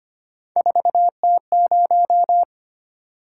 Morse code, heard as a keyed tone, about 25 wpm, 700 Hz. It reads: 4T0
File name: Three Letter-Number Combinations - 25wpm